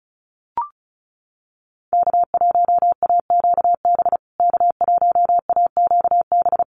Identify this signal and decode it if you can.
Morse code, heard as a keyed tone, about 34 wpm, 700 Hz. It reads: K1AQB K1AQB
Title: USA Call Signs Repeated 3x 35wpm